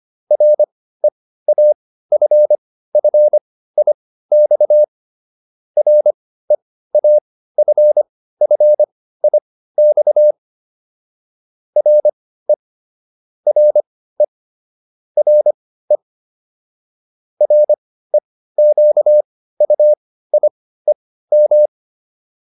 Morse code, heard as a keyed tone, about 25 wpm, spaced as for 15 wpm, 600 Hz. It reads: REAFFIX REAFFIX RE RE RE REQUIEM